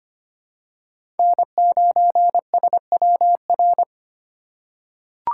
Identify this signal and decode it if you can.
Morse code, heard as a keyed tone, about 25 wpm, 700 Hz. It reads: N9SWR
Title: USA Call Signs - 5 Characters - 3x Character Spacing - 25wpm